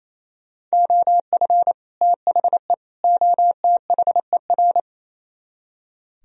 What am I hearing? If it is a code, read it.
Morse code, heard as a keyed tone, about 36 wpm, 700 Hz. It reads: OF THE OTHER